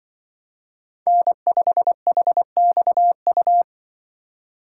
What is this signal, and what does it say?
Morse code, heard as a keyed tone, about 25 wpm, 700 Hz. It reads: N5HXU
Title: Speed Racing - USA Call Signs 20wpm